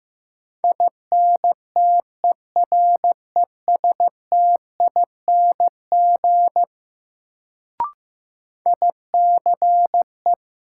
Morse code, heard as a keyed tone, about 16 wpm, 700 Hz. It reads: INTERESTING ICE
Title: Sentences from Top 1000 Words - New Word Drill - 15wpm